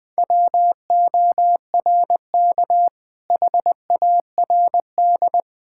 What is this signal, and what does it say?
Morse code, heard as a keyed tone, about 20 wpm, 700 Hz. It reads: WORK HARD
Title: Sentences with 2-Word Phrases - 20wpm